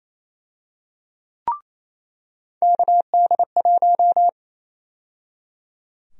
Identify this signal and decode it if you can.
Morse code, heard as a keyed tone, about 28 wpm, 700 Hz. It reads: KD1
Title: USA Call Sign Prefixes - 28wpm